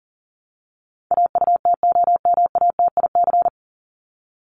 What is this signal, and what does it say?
Morse code, heard as a keyed tone, about 40 wpm, 700 Hz. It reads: AUTOMATIC
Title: Top 1751-2000 Words 40wpm